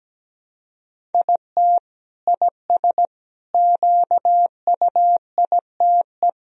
Morse code, heard as a keyed tone, about 17 wpm, 700 Hz. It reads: IT IS QUITE
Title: Sentences from Top 1000 Words - Review Part 3 - 17wpm